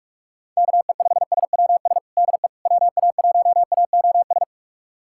Morse code, heard as a keyed tone, about 46 wpm, 700 Hz. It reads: KE5SWS DE WA1AOS
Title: USA Callsign DE Callsign - 6 Characters - 45wpm